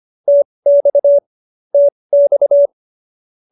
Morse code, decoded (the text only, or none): TX TX